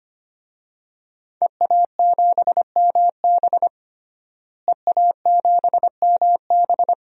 Morse code, heard as a keyed tone, about 25 wpm, 700 Hz. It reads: EA7MB EA7MB
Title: DXCC-100 Call Signs - 5 Characters - Repeated 2x - 25wpm